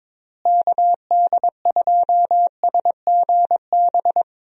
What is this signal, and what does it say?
Morse code, heard as a keyed tone, about 22 wpm, 700 Hz. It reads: KD2SGB
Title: USA Call Signs - 6 Characters - Repeated 2x - 22wpm